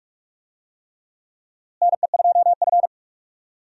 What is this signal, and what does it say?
Morse code, heard as a keyed tone, about 46 wpm, 700 Hz. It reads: NE2F
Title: USA Call Signs - 4 Characters - Repeated 3x - 45wpm